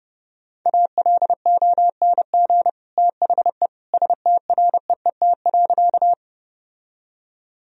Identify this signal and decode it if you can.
Morse code, heard as a keyed tone, about 30 wpm, 700 Hz. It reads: ALONG THE STREET.
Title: Sentences from Top 1000 Words - Review Part 3 - 30wpm